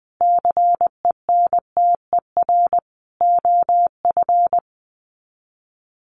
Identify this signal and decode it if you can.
Morse code, heard as a keyed tone, about 20 wpm, 700 Hz. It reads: CENTER OF
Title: Sets of 4 Words 20wpm